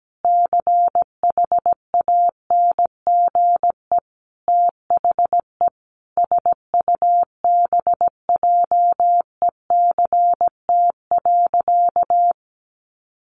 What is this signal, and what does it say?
Morse code, heard as a keyed tone, about 17 wpm, 700 Hz. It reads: CHANGE THE SUBJECT.
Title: Sentences from Top 800 Words 17wpm